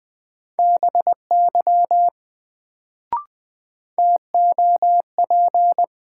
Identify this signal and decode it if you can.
Morse code, heard as a keyed tone, about 20 wpm, 700 Hz. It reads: BY TOP
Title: Sets of 2 Words - New Word Drill - 20wpm